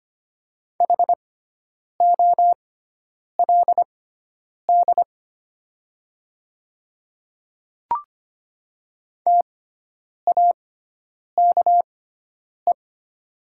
Morse code, heard as a keyed tone, about 25 wpm, spaced as for 9 wpm, 700 Hz. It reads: HOLD TAKE
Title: 6x Character Spacing - Top 500 Words - 25wpm